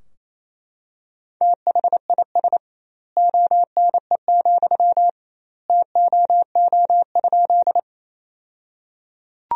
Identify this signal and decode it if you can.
Morse code, heard as a keyed tone, about 28 wpm, 700 Hz. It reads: THIS ONE, TOO?
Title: 2x Word-Spacing - Sentences from Top 200 Words - 28wpm